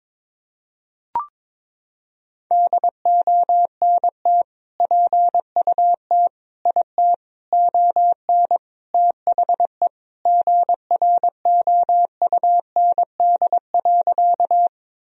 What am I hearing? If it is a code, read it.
Morse code, heard as a keyed tone, about 22 wpm, 700 Hz. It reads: DONT PUT IT ON THE GROUND.
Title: Sentences from Top 1000 Words - Review Part 2 - 22wpm